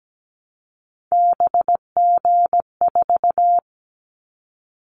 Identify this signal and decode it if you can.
Morse code, heard as a keyed tone, about 17 wpm, 700 Hz. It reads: BG4
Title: Callsign Prefixes for Top 100 DXCC - 17wpm